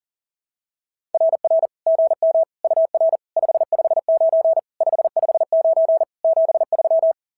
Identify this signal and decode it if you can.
Morse code, heard as a keyed tone, about 40 wpm, 660 Hz. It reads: RR GM UR 559 559 73
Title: SOTA QSO Practice 40wpm